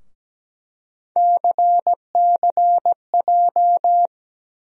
Morse code, heard as a keyed tone, about 17 wpm, 700 Hz. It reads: CCJ
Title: USA Call Sign Postfixes - 3 Letters 17wpm